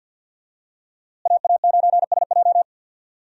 Morse code, heard as a keyed tone, about 48 wpm, 700 Hz. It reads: AA9SJ
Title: USA Call Signs - 5 Characters - Repeated 3x - 50wpm